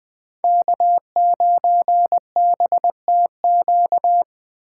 Morse code, heard as a keyed tone, about 21 wpm, 700 Hz. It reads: K9BTQ
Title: K1USN SST QSO Practice - 20wpm